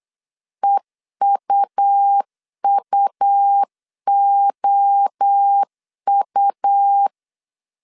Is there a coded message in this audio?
Yes: Morse code from 0.6 to 7.1 s, heard as a keyed tone, about 8 wpm, 800 Hz. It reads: EUUOU